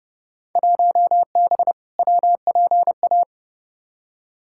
Morse code, heard as a keyed tone, about 31 wpm, 700 Hz. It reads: 1B WPA
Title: Field Day Exchanges - 30wpm